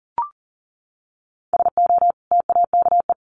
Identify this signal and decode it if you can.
Morse code, heard as a keyed tone, about 38 wpm, 700 Hz. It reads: SO TAKE
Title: Sentences from Top 300 Words 40wpm